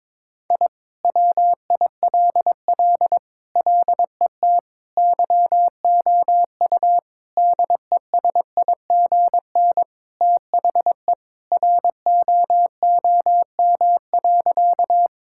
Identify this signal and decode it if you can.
Morse code, heard as a keyed tone, about 22 wpm, 700 Hz. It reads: I WILL LET YOU DESIGN THE ROOM.
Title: Sentences from Top 1000 Words - Review Part 5 - 22wpm